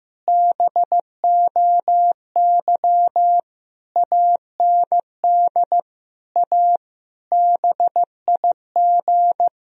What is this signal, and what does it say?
Morse code, heard as a keyed tone, about 16 wpm, 700 Hz. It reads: BOY AND A BIG
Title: Sentences from Top 1000 Words - Review Part 1 - 15wpm